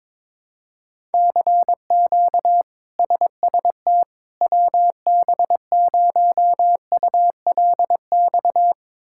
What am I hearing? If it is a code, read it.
Morse code, heard as a keyed tone, about 22 wpm, 700 Hz. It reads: CQ SST WB0ULX
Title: K1USN SST QSO Practice - 22wpm